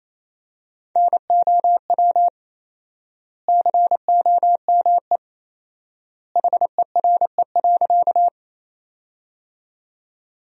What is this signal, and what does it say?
Morse code, heard as a keyed tone, about 28 wpm, 700 Hz. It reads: NOW COME HERE.
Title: 4x Word-Spacing - Sentences from Top 200 Words - 28wpm